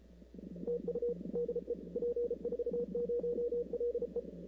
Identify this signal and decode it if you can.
Morse code, heard as a keyed tone, about 33 wpm, 480 Hz. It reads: TU DE P40LE
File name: CW / 2005-k5tr arrl cw dx -03:00